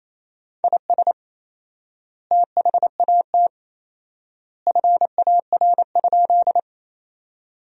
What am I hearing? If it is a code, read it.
Morse code, heard as a keyed tone, about 27 wpm, 700 Hz. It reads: IS THAT FAR?
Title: 4x Word-Spacing - Sentences from Top 300 Words - 28wpm